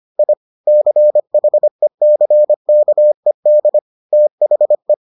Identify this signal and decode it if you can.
Morse code, heard as a keyed tone, about 25 wpm, 600 Hz. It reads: I CHECKED THE